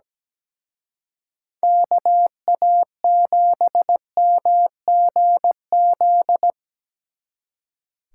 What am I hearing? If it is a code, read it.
Morse code, heard as a keyed tone, about 20 wpm, 700 Hz. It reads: KA7MGZ
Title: Speed Racing - USA 2x3 Call Signs 17wpm